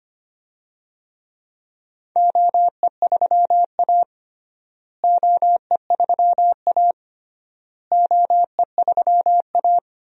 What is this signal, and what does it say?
Morse code, heard as a keyed tone, about 25 wpm, 700 Hz. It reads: OE3A OE3A OE3A
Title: DXCC-100 Call Signs - 4 Characters - Repeated 3x - 25wpm